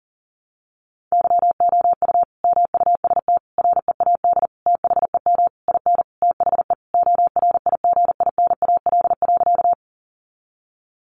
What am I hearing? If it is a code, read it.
Morse code, heard as a keyed tone, about 40 wpm, 700 Hz. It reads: YOU MUST READ THEM IN THE ORIGINAL.
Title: Sentences from Top 900 Words 40wpm